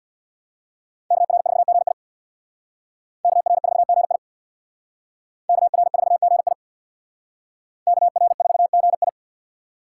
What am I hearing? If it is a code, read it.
Morse code, heard as a keyed tone, about 60 wpm, 700 Hz. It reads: KR4GI KR4GI KR4GI KR4GI
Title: Speed Racing - USA 2x2 Call Signs 50wpm